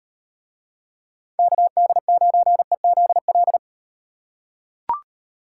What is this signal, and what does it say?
Morse code, heard as a keyed tone, about 36 wpm, 700 Hz. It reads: KD9EZL